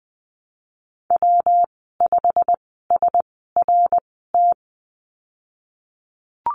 Morse code, heard as a keyed tone, about 20 wpm, spaced as for 15 wpm, 700 Hz. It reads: W5SRT
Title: 140 Lesson 23g - Letter f - Call Signs 20wpm